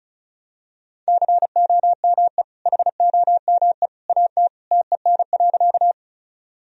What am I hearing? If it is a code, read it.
Morse code, heard as a keyed tone, about 34 wpm, 700 Hz. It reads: COME HOME AT TEN.
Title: Sentences from Top 1000 Words - Review Part 2 - 35wpm